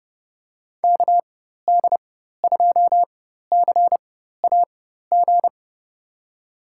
Morse code, heard as a keyed tone, about 30 wpm, spaced as for 14 wpm, 700 Hz. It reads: KD2CAG
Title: USA Call Signs - 4x Character Spacing - 30wpm